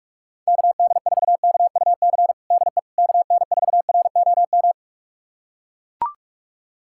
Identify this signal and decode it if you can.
Morse code, heard as a keyed tone, about 44 wpm, 690 Hz. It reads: KD4KUC DE KN4ROM
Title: USA Callsign DE Callsign 45wpm